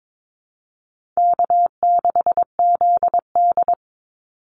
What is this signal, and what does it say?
Morse code, heard as a keyed tone, about 22 wpm, 700 Hz. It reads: K6ZD
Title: Call Signs - 22wpm